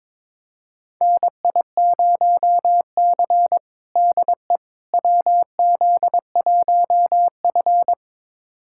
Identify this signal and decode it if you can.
Morse code, heard as a keyed tone, about 22 wpm, 700 Hz. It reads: NI0C DE WZ1F
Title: USA Callsign DE Callsign - 4 Characters - 22wpm